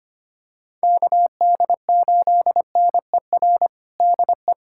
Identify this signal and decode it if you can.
Morse code, heard as a keyed tone, about 25 wpm, 700 Hz. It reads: KD8NER DE
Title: USA Callsign DE Callsign - 6 Characters - 25wpm